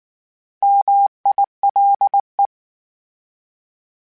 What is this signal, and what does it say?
Morse code, heard as a keyed tone, about 19 wpm, 800 Hz. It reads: MILE